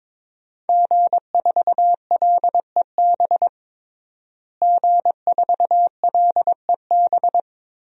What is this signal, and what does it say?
Morse code, heard as a keyed tone, about 22 wpm, 700 Hz. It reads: G4LEB G4LEB
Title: DXCC-100 Call Signs - 5 Characters - Repeated 3x - 22wpm